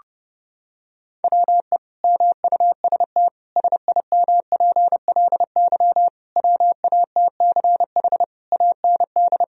Morse code, heard as a keyed tone, about 32 wpm, 700 Hz. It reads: WE MUST SIMPLY WATCH AND
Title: Sentences from Top 1000 Words - Review Part 4 - 30wpm